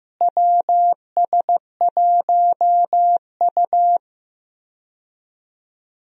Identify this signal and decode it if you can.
Morse code, heard as a keyed tone, about 15 wpm, 700 Hz. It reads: WS1U
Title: USA Call Signs - 4 Characters - Repeated 2x - 15wpm